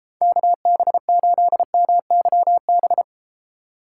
Morse code, heard as a keyed tone, about 33 wpm, 700 Hz. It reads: KB8MYB